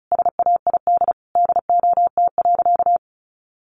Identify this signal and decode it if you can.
Morse code, heard as a keyed tone, about 34 wpm, 700 Hz. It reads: SAID DOT.
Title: Sentences from Top 1500 Words 35wpm